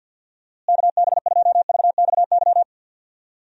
Morse code, heard as a keyed tone, about 50 wpm, 700 Hz. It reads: KB2VXY
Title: USA Call Signs - 6 Characters - Repeated 3x - 50wpm